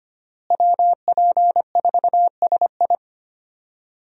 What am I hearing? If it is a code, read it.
Morse code, heard as a keyed tone, about 25 wpm, 700 Hz. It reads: WP4SI
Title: USA Call Signs - 5 Characters - Repeated 3x - 25wpm